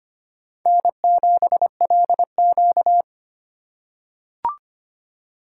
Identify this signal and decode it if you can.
Morse code, heard as a keyed tone, about 25 wpm, 700 Hz. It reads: N7LQ
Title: USA Call Signs - 4 Characters - 2x Character Spacing - 25wpm